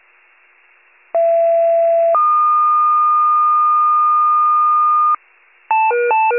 Two-tone page, heard as a one-second tone, then a three-second tone, about 670 then 1220 Hz. It is heard from 1.1 to 5.2 s.